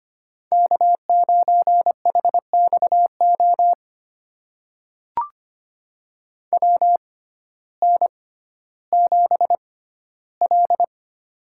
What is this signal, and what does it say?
Morse code, heard as a keyed tone, about 25 wpm, spaced as for 14 wpm, 700 Hz. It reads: K9HXO WN7L
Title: USA Call Signs - 6x Character Spacing - 25wpm